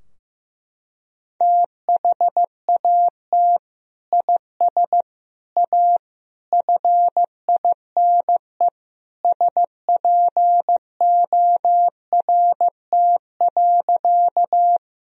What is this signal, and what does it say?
Morse code, heard as a keyed tone, about 15 wpm, 700 Hz. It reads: THAT IS A FINE SPORT.
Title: Sentences from Top 1000 Words - Review Part 4 - 15wpm